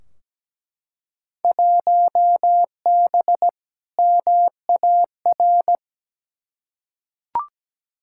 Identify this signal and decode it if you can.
Morse code, heard as a keyed tone, about 17 wpm, 700 Hz. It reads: 1B MAR
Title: Field Day Exchanges Encore - 17wpm